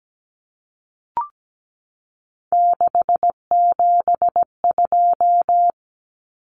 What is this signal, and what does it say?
Morse code, heard as a keyed tone, about 17 wpm, 700 Hz. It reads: 672
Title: Numbers - 3-Digit 17wpm